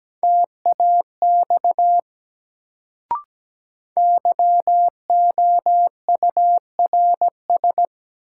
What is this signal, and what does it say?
Morse code, heard as a keyed tone, about 17 wpm, 700 Hz. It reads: TAX YOURS
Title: Sentences from Top 1500 Words - New Word Drill Alt - 17wpm